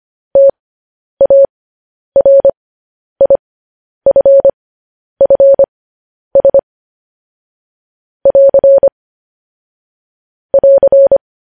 Morse code, heard as a keyed tone, about 25 wpm, spaced as for 10 wpm, 550 Hz. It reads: TARIFFS <AR> <AR>
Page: news headlines in morse code at 10 wpm